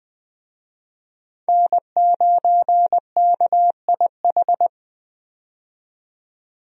Morse code, heard as a keyed tone, about 21 wpm, 700 Hz. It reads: N9KIH
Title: USA Call Signs - 5 Characters - 20wpm